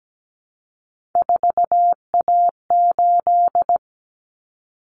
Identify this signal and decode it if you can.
Morse code, heard as a keyed tone, about 17 wpm, 700 Hz. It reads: 4A8
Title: Common Cut Numbers - 3-Digit 17wpm